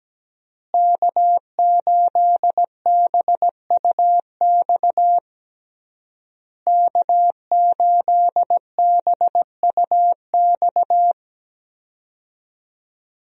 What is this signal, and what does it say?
Morse code, heard as a keyed tone, about 17 wpm, 700 Hz. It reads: K8BUX K8BUX
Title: USA Call Signs - 5 Characters - Repeated 3x - 17wpm